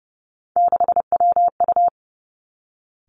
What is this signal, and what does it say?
Morse code, heard as a keyed tone, about 30 wpm, 700 Hz. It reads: 6WU
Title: Three Challenging Letter-Number Combinations 30wpm